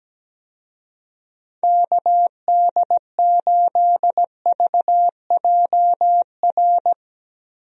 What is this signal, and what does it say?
Morse code, heard as a keyed tone, about 17 wpm, 700 Hz. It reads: KD8VJR
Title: USA Call Signs - 6 Characters - Repeated 2x - 17wpm